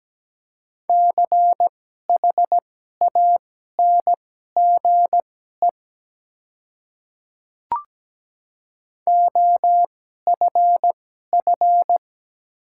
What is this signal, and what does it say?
Morse code, heard as a keyed tone, about 17 wpm, spaced as for 12 wpm, 700 Hz. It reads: CHANGE OFF
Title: 2x Character Spacing - Top 300 Words - 17wpm